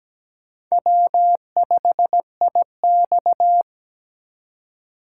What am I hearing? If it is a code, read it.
Morse code, heard as a keyed tone, about 20 wpm, 700 Hz. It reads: W5IX
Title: Speed Racing - USA 1x2 Call Signs 17wpm